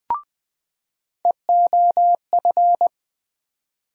Morse code, heard as a keyed tone, about 20 wpm, 700 Hz. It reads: EOF
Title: USA Call Sign Postfixes - 3 Letters 20wpm